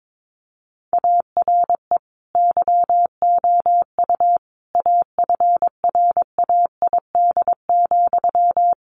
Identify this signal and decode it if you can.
Morse code, heard as a keyed tone, about 22 wpm, 700 Hz. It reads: ARE YOU AFRAID,
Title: Sentences from Top 1500 Words 22wpm